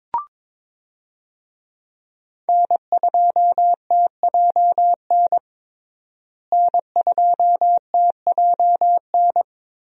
Morse code, heard as a keyed tone, about 23 wpm, 700 Hz. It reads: N2TJN N2TJN
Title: USA Call Signs - 5 Characters - Repeated 3x - 22wpm